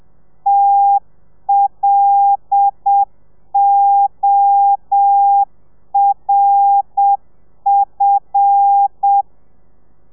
Morse code, heard as a keyed tone, about 7 wpm, 800 Hz. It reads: TLORF
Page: Lezione 1: audio a 30 CPM